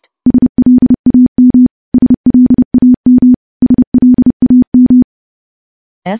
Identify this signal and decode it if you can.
Morse code, heard as a keyed tone, about 30 wpm, 250 Hz. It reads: SLAM SLAM SLAM
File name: morse / morse-training-30WPM